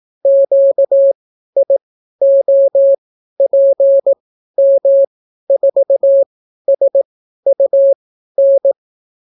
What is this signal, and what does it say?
Morse code, heard as a keyed tone, about 18 wpm, spaced as for 12 wpm, 550 Hz. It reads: QIOPM4SUN